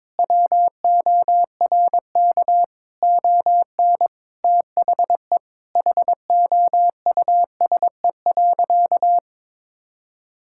Morse code, heard as a keyed tone, about 22 wpm, 700 Hz. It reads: WORK ON THE HOUSE.
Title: Sentences from Top 1000 Words - Review Part 5 - 22wpm